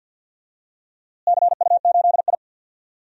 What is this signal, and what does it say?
Morse code, heard as a keyed tone, about 47 wpm, 700 Hz. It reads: CU8I
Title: DXCC-100 Call Signs - 4 Characters - Repeated 3x - 50wpm